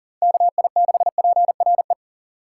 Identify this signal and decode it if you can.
Morse code, heard as a keyed tone, about 37 wpm, 700 Hz. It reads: KI6PRE
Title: USA Call Signs - 6 Characters - 3x Character Spacing - 40wpm